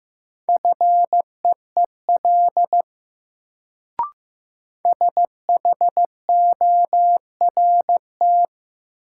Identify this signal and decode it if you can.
Morse code, heard as a keyed tone, about 15 wpm, 700 Hz. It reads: FEEL SHORT
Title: Sentences from Top 400 Words - New Word Drill - 15wpm